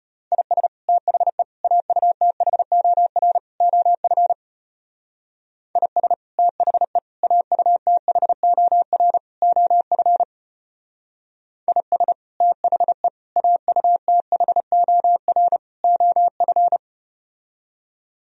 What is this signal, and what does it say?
Morse code, heard as a keyed tone, about 33 wpm, 700 Hz. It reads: IS THE AUTHOR OF IS THE AUTHOR OF IS THE AUTHOR OF